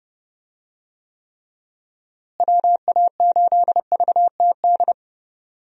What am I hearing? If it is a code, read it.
Morse code, heard as a keyed tone, about 32 wpm, 700 Hz. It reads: WA8VTD
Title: USA Call Signs - 6 Characters - Repeated 2x - 30wpm